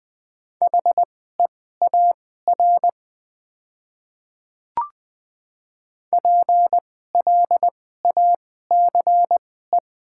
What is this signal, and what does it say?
Morse code, heard as a keyed tone, about 20 wpm, spaced as for 14 wpm, 700 Hz. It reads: HEAR PLACE
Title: 2x Character Spacing - Top 500 Words - 20wpm